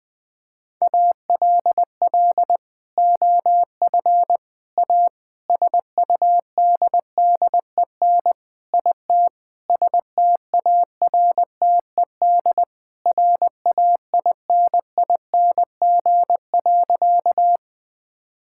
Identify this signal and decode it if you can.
Morse code, heard as a keyed tone, about 21 wpm, 700 Hz. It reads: ALL OF A SUDDEN IT STARTED RAINING.
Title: Sentences with 4-Word Phrases - 20wpm